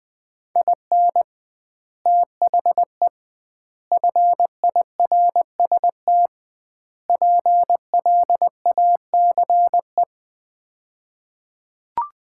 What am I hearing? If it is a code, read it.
Morse code, heard as a keyed tone, about 20 wpm, 700 Hz. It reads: IN THE FIRST PLACE
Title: 2x Word-Spacing - Sets of 4 Words - 20wpm